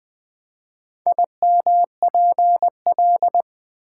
Morse code, heard as a keyed tone, about 20 wpm, 700 Hz. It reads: IMPL
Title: Build A Word - 4 Letters At A Time - 11-Letter Words - 20wpm